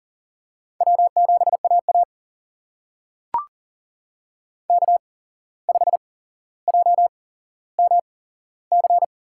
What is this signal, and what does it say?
Morse code, heard as a keyed tone, about 40 wpm, spaced as for 16 wpm, 700 Hz. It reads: W7AA K5JMC
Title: USA Call Signs - 8x Character Spacing - 40wpm